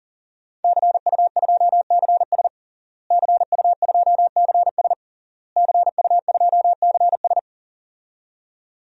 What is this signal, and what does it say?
Morse code, heard as a keyed tone, about 37 wpm, 700 Hz. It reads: CU2CS CU2CS CU2CS
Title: DXCC-100 Call Signs - 5 Characters - Repeated 3x - 40wpm